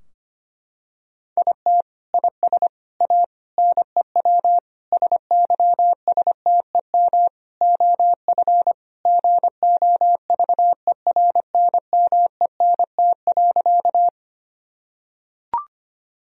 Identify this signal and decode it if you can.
Morse code, heard as a keyed tone, about 25 wpm, 700 Hz. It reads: IT IS A NEW SYSTEM OF GOVERNMENT.
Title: Sentences from Top 1000 Words - Review Part 3 - 25wpm